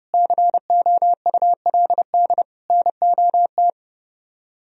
Morse code, heard as a keyed tone, about 38 wpm, 700 Hz. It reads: COULD NOT